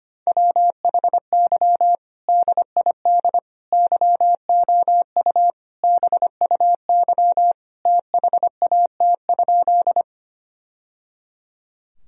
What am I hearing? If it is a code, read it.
Morse code, heard as a keyed tone, about 25 wpm, 700 Hz. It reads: WHY DID YOU BUY THAT?
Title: Sentences from Top 1000 Words - Review Part 4 - 25wpm